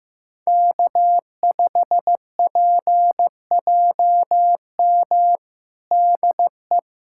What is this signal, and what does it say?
Morse code, heard as a keyed tone, about 16 wpm, 700 Hz. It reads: K5PJM DE
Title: USA Callsign DE Callsign - 5 Characters - 15wpm